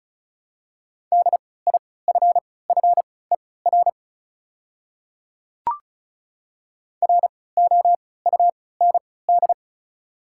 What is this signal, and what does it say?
Morse code, heard as a keyed tone, about 36 wpm, spaced as for 20 wpm, 700 Hz. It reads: DIFFER ROUND
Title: 3x Character Spacing - Top 200 Words - 35wpm